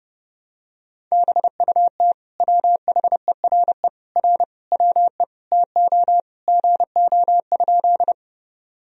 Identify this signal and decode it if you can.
Morse code, heard as a keyed tone, about 30 wpm, 700 Hz. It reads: BUT WHERE R WE TO GO?
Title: Sentences from Top 200 Words and Abbreviations 30wpm